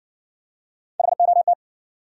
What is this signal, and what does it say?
Morse code, heard as a keyed tone, about 58 wpm, 700 Hz. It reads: HOT